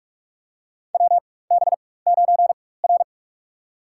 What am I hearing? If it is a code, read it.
Morse code, heard as a keyed tone, about 44 wpm, spaced as for 21 wpm, 700 Hz. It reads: WB9R
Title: USA Call Signs - 4 Characters - 4x Character Spacing - 45wpm